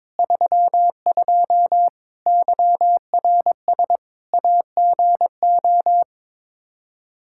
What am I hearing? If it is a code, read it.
Morse code, heard as a keyed tone, about 21 wpm, 700 Hz. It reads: 32 YRS AGO
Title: Practice QSO - Exchange 3 - More Info - 22wpm